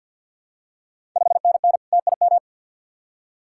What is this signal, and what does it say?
Morse code, heard as a keyed tone, about 53 wpm, 690 Hz. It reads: 5NN TIM
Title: Ontario QSO Party Exchange 50wpm